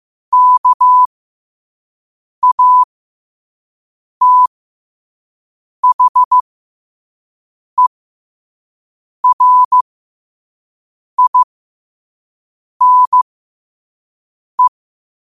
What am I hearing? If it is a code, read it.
Morse code, heard as a keyed tone, about 15 wpm, spaced as for 5 wpm, 1000 Hz. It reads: KATHERINE
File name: qotdfiles / 5wpm